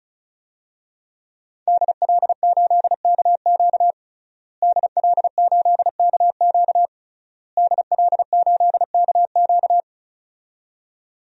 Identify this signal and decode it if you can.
Morse code, heard as a keyed tone, about 34 wpm, 700 Hz. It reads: DL8KQ DL8KQ DL8KQ
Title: DXCC-100 Call Signs - 5 Characters - Repeated 3x - 35wpm